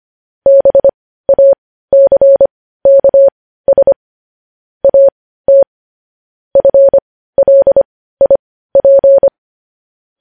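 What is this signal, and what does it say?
Morse code, heard as a keyed tone, about 25 wpm, spaced as for 15 wpm, 550 Hz. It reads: BACKS AT FLIP
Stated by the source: Morse code news headlines sent at 15, preceded by vvv vvv.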